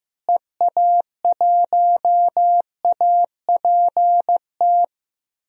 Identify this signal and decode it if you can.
Morse code, heard as a keyed tone, about 15 wpm, 700 Hz. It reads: EA1APT
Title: DXCC-100 Call Signs - 6 Characters - Repeated 3x - 15wpm